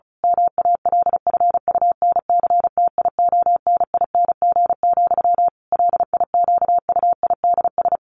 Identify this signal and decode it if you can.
Morse code, heard as a keyed tone, about 34 wpm, 700 Hz. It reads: MALFUNCTIONING, LIQUIDS
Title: A Sound Of Thunder 35wpm